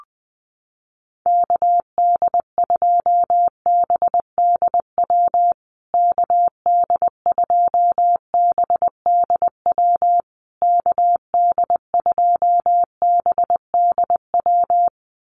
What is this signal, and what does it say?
Morse code, heard as a keyed tone, about 20 wpm, 700 Hz. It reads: KD2BDW KD2BDW KD2BDW
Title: USA Call Signs Repeated 3x 20wpm